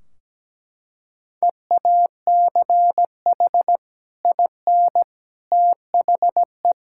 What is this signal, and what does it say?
Morse code, heard as a keyed tone, about 17 wpm, 700 Hz. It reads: EACH IN THE
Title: Sentences from Top 1600 Words 17wpm